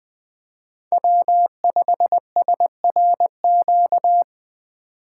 Morse code, heard as a keyed tone, about 20 wpm, 700 Hz. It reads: W5SRQ
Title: USA Call Signs - 8x Character Spacing - 20wpm